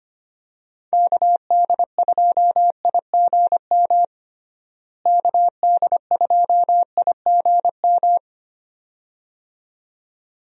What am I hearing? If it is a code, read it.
Morse code, heard as a keyed tone, about 25 wpm, 700 Hz. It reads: KD2IGM KD2IGM
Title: USA Call Signs - 6 Characters - Repeated 2x - 25wpm